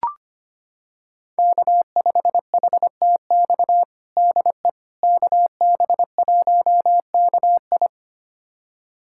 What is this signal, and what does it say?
Morse code, heard as a keyed tone, about 25 wpm, 700 Hz. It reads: K5HTX DE KB1KI
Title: USA Callsign DE Callsign - 5 Characters - 25wpm